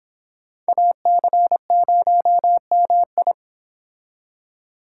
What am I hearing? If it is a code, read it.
Morse code, heard as a keyed tone, about 25 wpm, 700 Hz. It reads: AC0MI